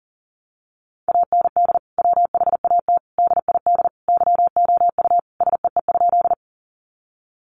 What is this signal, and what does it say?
Morse code, heard as a keyed tone, about 40 wpm, 700 Hz. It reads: AND WHAT DID YOU SEE?
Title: Sentences from Top 100 Words 40wpm